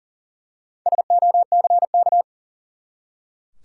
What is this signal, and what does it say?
Morse code, heard as a keyed tone, about 40 wpm, 700 Hz. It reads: SOCK
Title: Four Letter Words - 40wpm